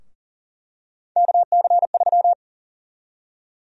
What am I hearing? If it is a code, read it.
Morse code, heard as a keyed tone, about 40 wpm, 700 Hz. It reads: KC3
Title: USA Call Sign Prefixes - Repeated 2x - 40wpm